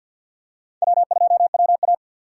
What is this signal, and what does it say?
Morse code, heard as a keyed tone, about 51 wpm, 690 Hz. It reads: W2WA
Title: USA Call Signs - 6x Character Spacing - 50wpm